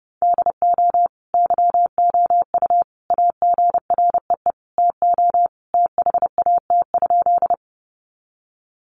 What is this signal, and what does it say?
Morse code, heard as a keyed tone, about 30 wpm, 700 Hz. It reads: DO YOU AGREE TO THAT?
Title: Sentences from Top 1100 Words 30wpm